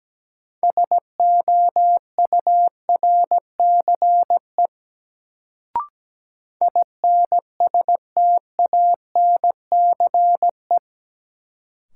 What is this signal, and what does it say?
Morse code, heard as a keyed tone, about 17 wpm, 700 Hz. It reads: SOURCE INSTANCE
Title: Sentences from Top 800 Words - New Word Drill - 17wpm